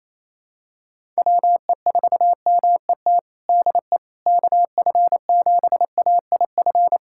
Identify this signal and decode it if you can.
Morse code, heard as a keyed tone, about 28 wpm, 700 Hz. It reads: WE4MET DE KF7AIF
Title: USA Callsign DE Callsign - 6 Characters - 28wpm